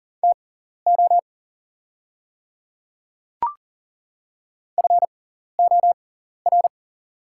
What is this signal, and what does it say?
Morse code, heard as a keyed tone, about 40 wpm, spaced as for 14 wpm, 690 Hz. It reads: TO FOR
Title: 6x Character Spacing - Top 200 Words - 40wpm